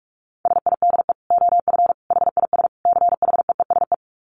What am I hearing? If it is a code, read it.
Morse code, heard as a keyed tone, about 44 wpm, 700 Hz. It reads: SIDE OF HIS CHEESE
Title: Dit Challenge 45wpm